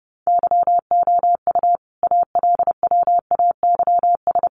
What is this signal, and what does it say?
Morse code, heard as a keyed tone, about 30 wpm, 700 Hz. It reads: YOU ALWAYS
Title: Sentences from Top 700 Words 30wpm